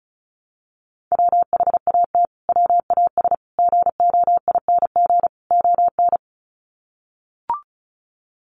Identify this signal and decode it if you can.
Morse code, heard as a keyed tone, about 34 wpm, 700 Hz. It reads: WHAT WAS GOING ON
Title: Sets of 4 Words 35wpm